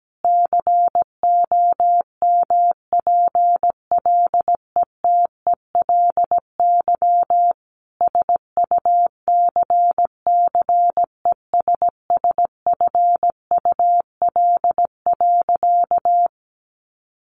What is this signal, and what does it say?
Morse code, heard as a keyed tone, about 17 wpm, 700 Hz. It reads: COMPLETELY SUCCESSFUL.
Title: Sentences from Top 1100 Words 17wpm